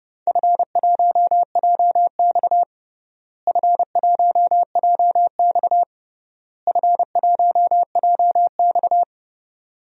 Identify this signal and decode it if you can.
Morse code, heard as a keyed tone, about 30 wpm, 700 Hz. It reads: F1JX F1JX F1JX
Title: DXCC-100 Call Signs - 4 Characters - Repeated 3x - 30wpm